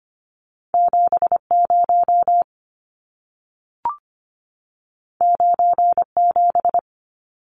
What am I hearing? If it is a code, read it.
Morse code, heard as a keyed tone, about 25 wpm, 700 Hz. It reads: 70 97
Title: Numbers - 2-Digit 25wpm